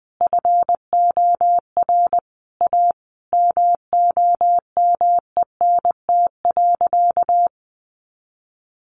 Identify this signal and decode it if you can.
Morse code, heard as a keyed tone, about 19 wpm, 700 Hz. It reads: FOR A MOMENT.
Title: Sentences from Top 1500 Words 20wpm